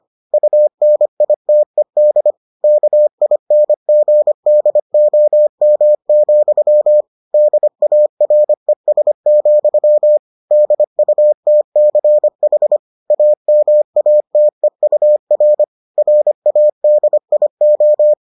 Morse code, heard as a keyed tone, about 25 wpm, 600 Hz. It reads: UNITED KINGDOM, DARES, DUTCH AMATEUR RADIO